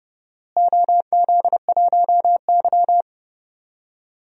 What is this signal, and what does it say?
Morse code, heard as a keyed tone, about 30 wpm, 700 Hz. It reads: OZ1Y
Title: DXCC-100 Call Signs - 4 Characters - Repeated 2x - 30wpm